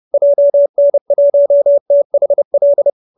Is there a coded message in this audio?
Yes: Morse code from 0.1 to 2.9 s, heard as a keyed tone, about 30 wpm, 570 Hz. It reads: JN1THL